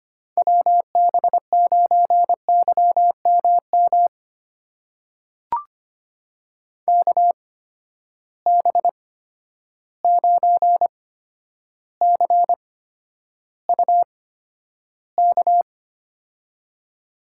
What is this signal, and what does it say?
Morse code, heard as a keyed tone, about 25 wpm, spaced as for 11 wpm, 700 Hz. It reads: WB9YMM KB9CUK
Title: USA Call Signs - 6 Characters - 8x Character Spacing - 25wpm